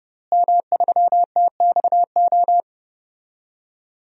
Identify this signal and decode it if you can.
Morse code, heard as a keyed tone, about 30 wpm, 700 Hz. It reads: M3TXO
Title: DXCC-100 Call Signs - 5 Characters - Repeated 2x - 30wpm